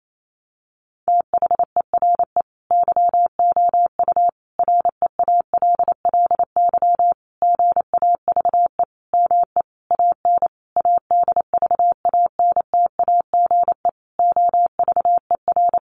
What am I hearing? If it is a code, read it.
Morse code, heard as a keyed tone, about 28 wpm, 700 Hz. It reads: THERE YOU REALLY GAVE ME AN ADVANTAGE OVER
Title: Sentences from Top 900 Words 28wpm